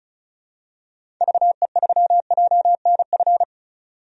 Morse code, heard as a keyed tone, about 36 wpm, 700 Hz. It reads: VE3JNF
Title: DXCC-100 Call Signs - 6 Characters - 35wpm